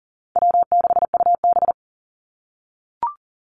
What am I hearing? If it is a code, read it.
Morse code, heard as a keyed tone, about 40 wpm, 700 Hz. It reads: W6UB
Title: NA SOTA Activator Callsigns 40wpm